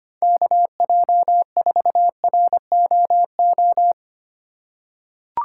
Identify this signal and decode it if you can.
Morse code, heard as a keyed tone, about 25 wpm, 700 Hz. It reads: KJ4ROO
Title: USA Call Signs - 6 Characters - 2x Character Spacing - 25wpm